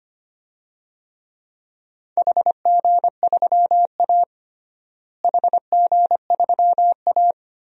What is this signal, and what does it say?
Morse code, heard as a keyed tone, about 25 wpm, 700 Hz. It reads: HG3A HG3A
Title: DXCC-100 Call Signs - 4 Characters - Repeated 3x - 25wpm